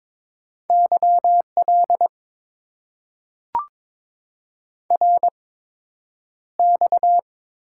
Morse code, heard as a keyed tone, about 22 wpm, spaced as for 9 wpm, 700 Hz. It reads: YL RX
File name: Top 90 QSO Elements - 8x Character Spacing - 22wpm